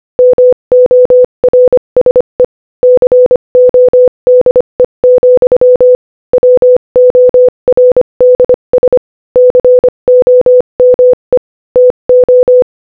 Morse code, heard as a keyed tone, about 25 wpm, 500 Hz. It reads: MORSE CODE, WORDS COME TO
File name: Morse-Code-Audio-Speed-25-9